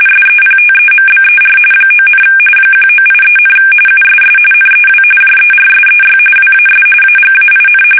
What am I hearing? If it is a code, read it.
RTTY, about 45.45 baud. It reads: SEPTEMBER 9, 1983, USS NEW JERSEY WAS AGAIN
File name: fsk850 / battleshipnj